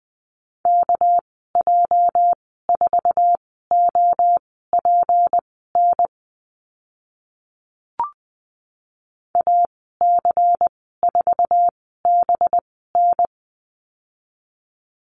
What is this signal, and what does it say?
Morse code, heard as a keyed tone, about 20 wpm, spaced as for 15 wpm, 700 Hz. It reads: KJ4OPN AC4BN
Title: 207 Lesson 33g - Letter j - Call Signs 20wpm